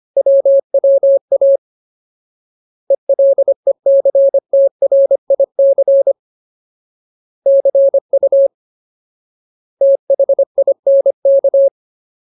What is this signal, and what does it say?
Morse code, heard as a keyed tone, about 25 wpm, 550 Hz. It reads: WWA ELECTRIC CU THINK
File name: Ultimate Word List / 25WPM Ultimate